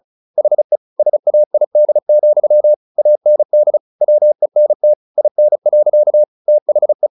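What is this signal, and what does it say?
Morse code, heard as a keyed tone, about 34 wpm, 600 Hz. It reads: HE SAID, AND WENT IN. THE
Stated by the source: Morse practice at 35 wpm from CW med Gnister, sending Ken Follett's novel Never in English.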